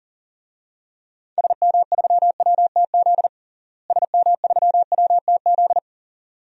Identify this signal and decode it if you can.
Morse code, heard as a keyed tone, about 40 wpm, 700 Hz. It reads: SM3WTZ SM3WTZ
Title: DXCC-100 Call Signs - 6 Characters - Repeated 2x - 40wpm